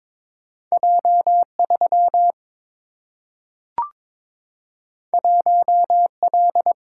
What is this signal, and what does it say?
Morse code, heard as a keyed tone, about 22 wpm, 700 Hz. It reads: J 3 1 L
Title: Single-Letter Single-Number 22wpm